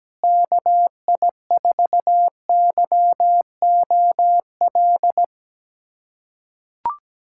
Morse code, heard as a keyed tone, about 17 wpm, 700 Hz. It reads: KI4YOL
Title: USA Callsign DE Callsign - 6 Characters - 17wpm